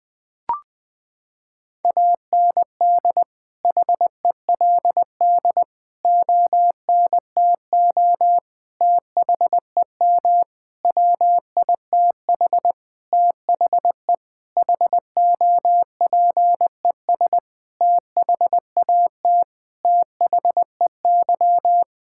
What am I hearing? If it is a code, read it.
Morse code, heard as a keyed tone, about 19 wpm, 700 Hz. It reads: AND HELD ONTO THEM WITH THE HOPES THAT THEY
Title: Sentences with 9-Word Phrases - 20wpm